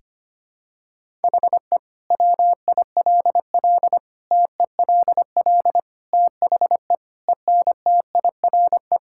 Morse code, heard as a keyed tone, about 25 wpm, 700 Hz. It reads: HE WILL TELL THE ENTIRE
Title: Sentences from Top 1000 Words - Review Part 5 - 25wpm